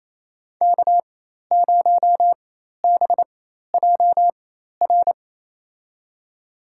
Morse code, heard as a keyed tone, about 28 wpm, spaced as for 13 wpm, 700 Hz. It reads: K0BJR
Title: USA Call Signs - 4x Character Spacing - 28wpm